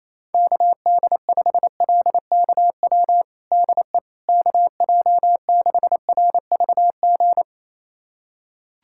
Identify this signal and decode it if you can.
Morse code, heard as a keyed tone, about 28 wpm, 700 Hz. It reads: KD5LKW DE KJ6RVG
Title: USA Callsign DE Callsign - 6 Characters - 28wpm